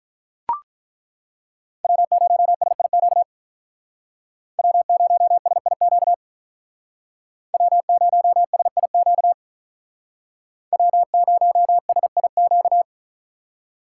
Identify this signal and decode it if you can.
Morse code, heard as a keyed tone, about 42 wpm, 700 Hz. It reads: W0SIQ W0SIQ W0SIQ W0SIQ
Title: Speed Racing - USA 1x3 Call Signs 35wpm